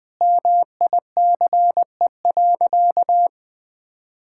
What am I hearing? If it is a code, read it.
Morse code, heard as a keyed tone, about 20 wpm, 700 Hz. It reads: MICE.
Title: Easy Sentences with Hints - 20wpm